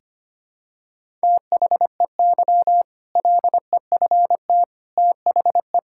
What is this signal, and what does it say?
Morse code, heard as a keyed tone, about 25 wpm, 700 Hz. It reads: THEY LEFT THE
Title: Sentences from Top 1000 Words - Review Part 5 - 25wpm